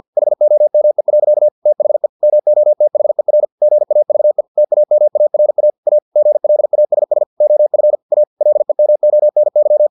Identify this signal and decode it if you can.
Morse code, heard as a keyed tone, about 50 wpm, 600 Hz. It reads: HOME. THE MOTHER GAVE TAMARA A GLASS OF A LEMONY
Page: Never 49wpm